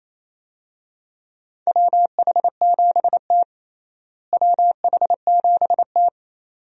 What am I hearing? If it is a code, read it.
Morse code, heard as a keyed tone, about 28 wpm, 700 Hz. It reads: WH7T WH7T
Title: USA Call Signs - 4 Characters - Repeated 2x - 28wpm